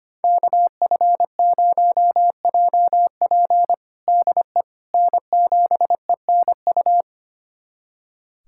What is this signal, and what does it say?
Morse code, heard as a keyed tone, about 25 wpm, 700 Hz. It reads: KF0JP DE N7ENU
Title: USA Callsign DE Callsign - 5 Characters - 25wpm